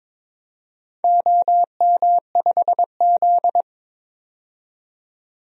Morse code, heard as a keyed tone, about 21 wpm, 700 Hz. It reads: OM5Z
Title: DXCC-100 Call Signs - 4 Characters - 22wpm